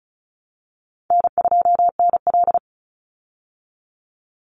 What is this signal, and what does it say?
Morse code, heard as a keyed tone, about 34 wpm, 700 Hz. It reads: N2NL
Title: Call Signs Repeated 3x - 35wpm